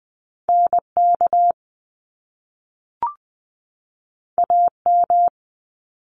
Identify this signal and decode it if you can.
Morse code, heard as a keyed tone, about 20 wpm, 700 Hz. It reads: NK AM
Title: Top Two Letter Combinations in Words 20wpm